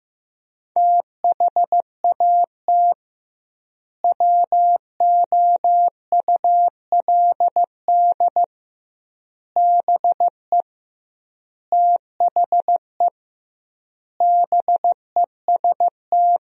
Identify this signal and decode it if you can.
Morse code, heard as a keyed tone, about 16 wpm, 700 Hz. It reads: THAT WOULD BE THE BEST
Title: 2x Word-Spacing - Sentences from Top 400 Words - 15wpm